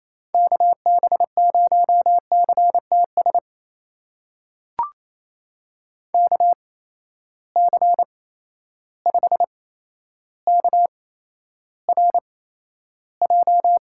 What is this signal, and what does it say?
Morse code, heard as a keyed tone, about 28 wpm, spaced as for 12 wpm, 700 Hz. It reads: KB0CTS KC5KRJ
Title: USA Call Signs - 6 Characters - 8x Character Spacing - 28wpm